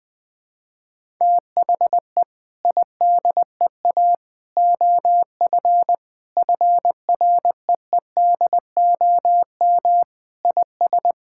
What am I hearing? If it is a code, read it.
Morse code, heard as a keyed tone, about 20 wpm, 700 Hz. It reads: THE IDEA OF FREEDOM IS
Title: Sentences with 3-Word Phrases - Spoken Beginning - 20wpm